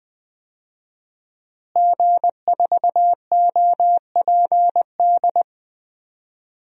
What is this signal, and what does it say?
Morse code, heard as a keyed tone, about 20 wpm, 700 Hz. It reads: G4OPD
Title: DXCC-100 Call Signs - 5 Characters - Repeated 3x - 20wpm